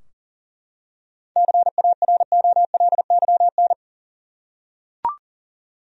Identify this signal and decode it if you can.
Morse code, heard as a keyed tone, about 40 wpm, 700 Hz. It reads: CAROLYN
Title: USA Common Names - 40wpm